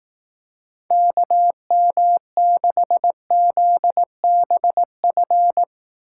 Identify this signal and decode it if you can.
Morse code, heard as a keyed tone, about 18 wpm, 700 Hz. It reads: KM6ZBF
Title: Speed Racing - USA 2x3 Call Signs 15wpm